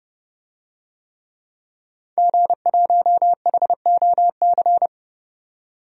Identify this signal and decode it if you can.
Morse code, heard as a keyed tone, about 32 wpm, 700 Hz. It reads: G1HOC
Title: DXCC-100 Call Signs - 5 Characters - 30wpm